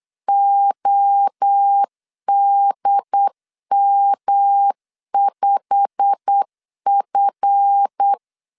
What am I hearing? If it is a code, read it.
Morse code, heard as a keyed tone, about 8 wpm, 800 Hz. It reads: ODM5F
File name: LEZIONI / 8 YJ45 / 8 YJ45 8wpm 3214